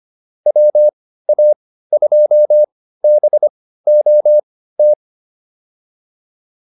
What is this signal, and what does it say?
Morse code, heard as a keyed tone, about 25 wpm, spaced as for 15 wpm, 600 Hz. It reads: WA2BOT